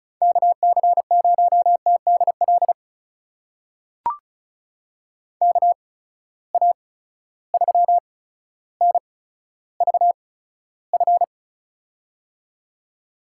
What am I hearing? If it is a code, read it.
Morse code, heard as a keyed tone, about 34 wpm, spaced as for 15 wpm, 700 Hz. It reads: KC0TDL KA3NVF
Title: USA Call Signs - 6 Characters - 8x Character Spacing - 35wpm